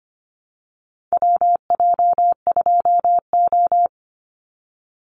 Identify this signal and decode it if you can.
Morse code, heard as a keyed tone, about 25 wpm, 700 Hz. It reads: WJ2O
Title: Call Signs Repeated 3x - 25wpm